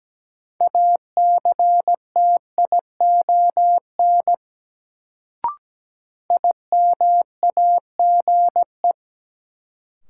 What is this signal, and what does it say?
Morse code, heard as a keyed tone, about 17 wpm, 700 Hz. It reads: ACTION IMAGE
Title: Sentences from Top 700 Words - New Word Drill - 17wpm